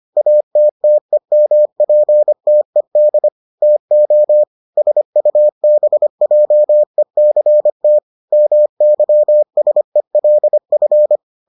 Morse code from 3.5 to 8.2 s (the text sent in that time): TO SUBJECT